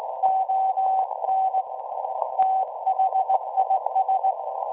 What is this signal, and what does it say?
Morse code, heard as a keyed tone, about 18 wpm, 760 Hz. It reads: ON THIS
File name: WebSDR records / SAQ at PI4THT part od 1005z 17-2kHz